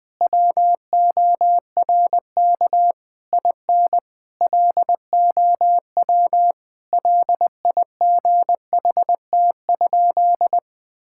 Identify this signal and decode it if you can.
Morse code, heard as a keyed tone, about 21 wpm, 700 Hz. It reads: WORK IN LOW LIGHT?
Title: Sentences from Top 1000 Words - Review Part 5 - 20wpm